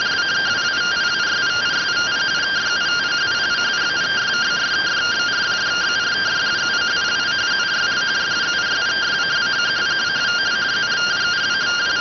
Two-tone teleprinter signal, about 170 Hz shift.